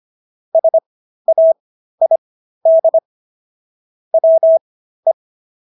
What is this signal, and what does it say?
Morse code, heard as a keyed tone, about 25 wpm, spaced as for 13 wpm, 650 Hz. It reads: SAID WE